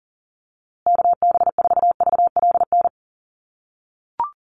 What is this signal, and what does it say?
Morse code, heard as a keyed tone, about 40 wpm, 700 Hz. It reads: KB4VLN
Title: USA Call Signs Repeated 2x 40wpm